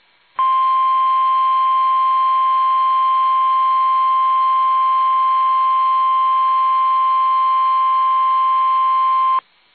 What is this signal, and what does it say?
Emergency Alert System (SAME) tones (the 1050 Hz attention tone). It is heard from 0.3 to 9.5 s.